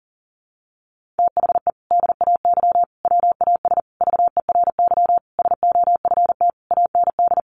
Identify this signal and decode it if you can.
Morse code, heard as a keyed tone, about 40 wpm, 700 Hz. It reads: THE DAY WAS VERY SOFT AND
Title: Sentences from Top 1500 Words 40wpm